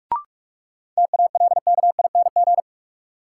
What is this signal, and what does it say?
Morse code, heard as a keyed tone, about 42 wpm, 700 Hz. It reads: TALKING
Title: Sets of 2 Words - New Word Drill - 45wpm